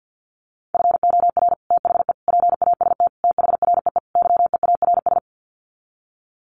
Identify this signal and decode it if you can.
Morse code, heard as a keyed tone, about 50 wpm, 700 Hz. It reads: FOR THE PAST THREE YEARS
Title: Sets of 5 Words Encore2 50wpm